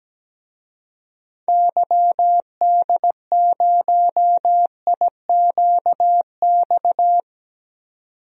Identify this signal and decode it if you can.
Morse code, heard as a keyed tone, about 17 wpm, 700 Hz. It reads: YD0IQX
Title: DXCC-100 Call Signs - 6 Characters - 17wpm